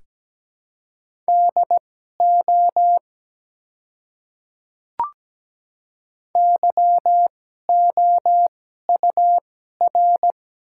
Morse code, heard as a keyed tone, about 17 wpm, spaced as for 12 wpm, 700 Hz. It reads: DO YOUR
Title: 2x Character Spacing - Top 100 Words - 17wpm